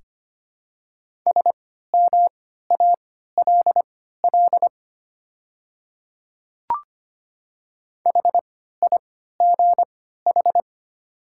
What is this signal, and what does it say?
Morse code, heard as a keyed tone, about 25 wpm, spaced as for 14 wpm, 700 Hz. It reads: SMALL HIGH
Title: 3x Character Spacing - Top 500 Words - 25wpm